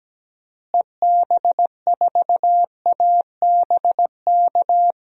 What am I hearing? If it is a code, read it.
Morse code, heard as a keyed tone, about 17 wpm, 700 Hz. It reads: EB4ABK
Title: DXCC-100 Call Signs - 6 Characters - Repeated 3x - 17wpm